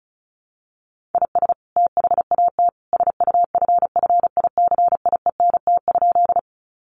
Morse code, heard as a keyed tone, about 34 wpm, 700 Hz. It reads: IS THAT SUFFICIENT?
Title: Sentences from Top 1400 Words 35wpm